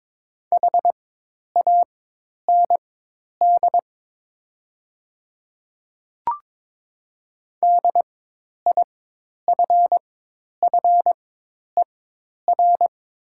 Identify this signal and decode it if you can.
Morse code, heard as a keyed tone, about 22 wpm, spaced as for 10 wpm, 700 Hz. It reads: HAND DIFFER
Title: 4x Character Spacing - Top 200 Words - 22wpm